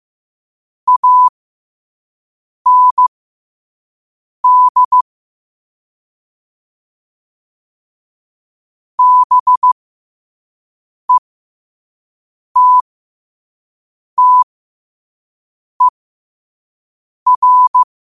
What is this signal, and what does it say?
Morse code, heard as a keyed tone, about 15 wpm, spaced as for 5 wpm, 1000 Hz. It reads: AND BETTER
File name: qotdfiles / 5wpm